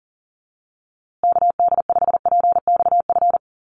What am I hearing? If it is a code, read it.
Morse code, heard as a keyed tone, about 40 wpm, 700 Hz. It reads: KD5PXF
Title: USA Call Signs Repeated 3x 40wpm